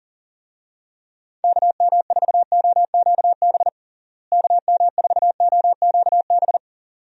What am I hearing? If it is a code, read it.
Morse code, heard as a keyed tone, about 40 wpm, 700 Hz. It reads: KM4OQB KM4OQB
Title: USA Call Signs - 6 Characters - Repeated 2x - 40wpm